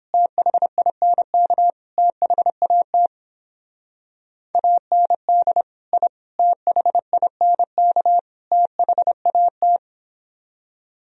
Morse code, heard as a keyed tone, about 29 wpm, 700 Hz. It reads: THINK THAT AND I THINK THAT